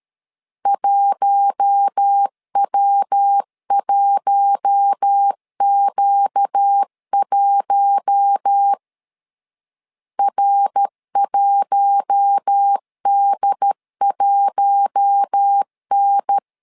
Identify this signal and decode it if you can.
Morse code, heard as a keyed tone, about 13 wpm, 800 Hz. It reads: 1W1Q1 R1D1N
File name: LEZIONI / 4 Q1 12wpm 2028